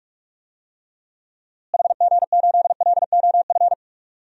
Morse code, heard as a keyed tone, about 43 wpm, 700 Hz. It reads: HG8LOF
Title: DXCC-100 Call Signs - 6 Characters - Repeated 2x - 45wpm